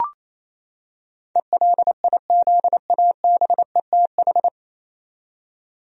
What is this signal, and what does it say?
Morse code, heard as a keyed tone, about 28 wpm, 700 Hz. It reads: ELIZABETH
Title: USA Common Names - 28wpm